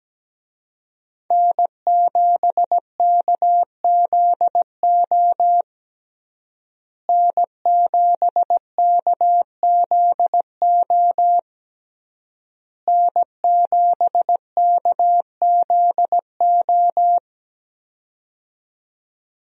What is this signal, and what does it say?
Morse code, heard as a keyed tone, about 17 wpm, 700 Hz. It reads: N7KZO N7KZO N7KZO
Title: USA Call Signs - 5 Characters - Repeated 3x - 17wpm